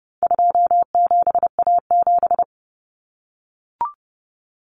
Morse code, heard as a keyed tone, about 30 wpm, 700 Hz. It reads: 27A7
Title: Common Cut Numbers - 4-Digit 30wpm